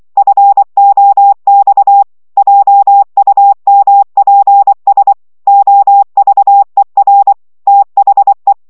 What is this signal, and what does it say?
Morse code, heard as a keyed tone, about 24 wpm, 800 Hz. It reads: FOX JUMPS OVER THE
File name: szamolo / CW 20 baud